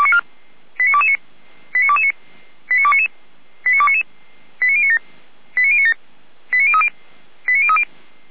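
Five-tone selective call (ZVEI), heard as quick runs of five tones, tones of about 70 ms. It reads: 89229 89229 89299 89299 89097 89097 89030 89030
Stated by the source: ZVEI five-tone call 89277 received on an SDR, logged by the Feuerwehr Orlamünde (fire brigade) server.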